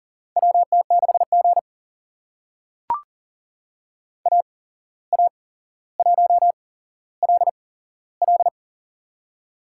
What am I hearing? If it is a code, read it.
Morse code, heard as a keyed tone, about 40 wpm, spaced as for 16 wpm, 700 Hz. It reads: WT6G AA1LL
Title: USA Call Signs - 8x Character Spacing - 40wpm